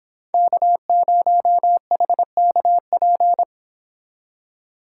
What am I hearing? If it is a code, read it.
Morse code, heard as a keyed tone, about 25 wpm, 700 Hz. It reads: K0HKP